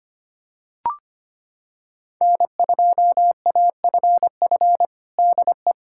Morse code, heard as a keyed tone, about 25 wpm, 700 Hz. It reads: N2AFF DE
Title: USA Callsign DE Callsign - 5 Characters - 25wpm